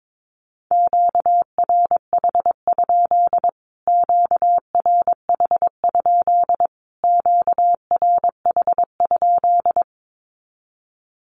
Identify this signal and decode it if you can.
Morse code, heard as a keyed tone, about 22 wpm, 700 Hz. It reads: QRH? QRH? QRH?
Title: Q-Codes Repeated 3x 22wpm